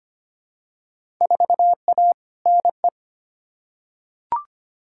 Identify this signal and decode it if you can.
Morse code, heard as a keyed tone, about 25 wpm, 700 Hz. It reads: 4A NE
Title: Field Day Exchanges - 25wpm